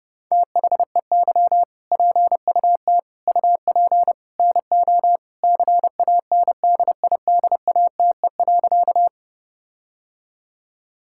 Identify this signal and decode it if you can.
Morse code, heard as a keyed tone, about 30 wpm, 700 Hz. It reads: THEY PUT UP NO CANDIDATE.
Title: Sentences from Top 1600 Words 30wpm